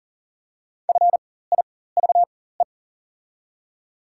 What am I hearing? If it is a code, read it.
Morse code, heard as a keyed tone, about 40 wpm, spaced as for 19 wpm, 690 Hz. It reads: FIVE
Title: 4x Character Spacing - Top 500 Words - 40wpm